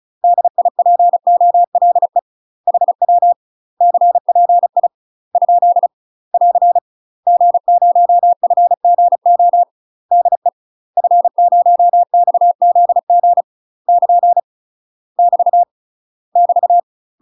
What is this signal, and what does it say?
Morse code, heard as a keyed tone, about 36 wpm, 700 Hz. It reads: DIPOLE HW CPI ? <AR> G0FGO DE F0XZG <KN> <BT> <BT>
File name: UK MASTER QSOs TRAINING/35WPM